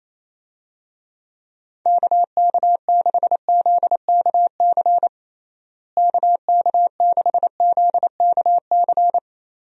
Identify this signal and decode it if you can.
Morse code, heard as a keyed tone, about 28 wpm, 700 Hz. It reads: KK6ZKC KK6ZKC
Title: USA Call Signs - 6 Characters - Repeated 3x - 28wpm